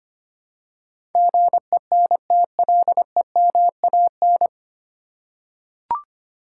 Morse code, heard as a keyed tone, about 25 wpm, 700 Hz. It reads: GENTLEMAN
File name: Nine Letter Words - 25wpm